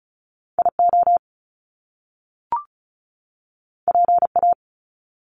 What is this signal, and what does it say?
Morse code, heard as a keyed tone, about 36 wpm, 700 Hz. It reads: IO PA
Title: Top Two Letter Combinations in Words 35wpm